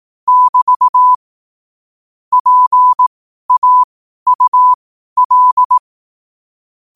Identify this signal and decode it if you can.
Morse code, heard as a keyed tone, about 18 wpm, spaced as for 12 wpm, 1000 Hz. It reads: <BT> PAUL
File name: qotdfiles / 12wpm